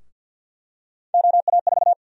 Morse code, heard as a keyed tone, about 51 wpm, 700 Hz. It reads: OA4
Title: USA Call Sign Prefixes - Repeated 2x - 50wpm